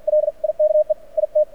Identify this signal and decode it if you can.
Morse code, heard as a keyed tone, about 50 wpm, spaced as for 42 wpm, 600 Hz. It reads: LIKE IT
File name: archived / CRSnet